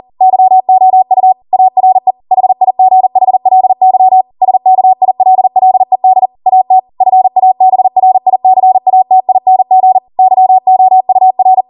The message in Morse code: YOU ARE HIGHLY SKILLED AT FABRICATING YOUR

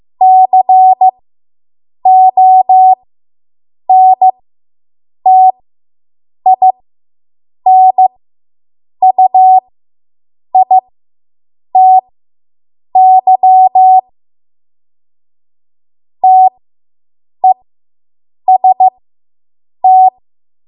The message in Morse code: CONTINUITY TEST

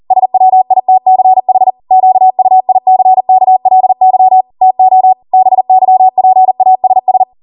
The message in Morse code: SWITCH QUICKLY TO BYPASS

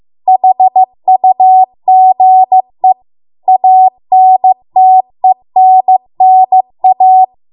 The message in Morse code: HUGE ANTENNA

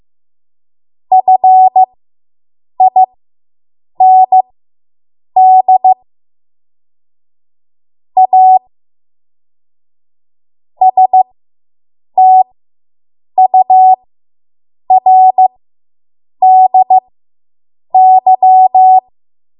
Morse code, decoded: FIND A STURDY

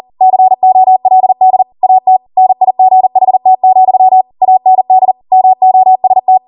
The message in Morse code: COLD AT NIGHT, AND MOST

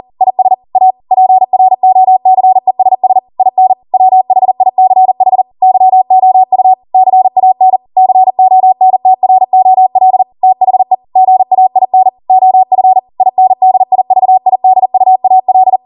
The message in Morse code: IS A PROCESS IN WHICH YOU CAN CONTROL THE GAIN OF INDIVIDUAL

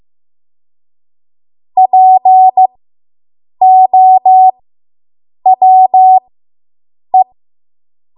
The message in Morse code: POWE

